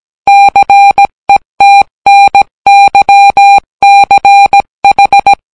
Morse code, heard as a keyed tone, about 17 wpm, 800 Hz. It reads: CETNYCH